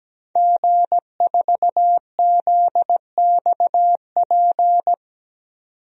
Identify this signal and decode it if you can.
Morse code, heard as a keyed tone, about 17 wpm, 700 Hz. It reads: G4ZXP
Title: DXCC-100 Call Signs - 5 Characters - 17wpm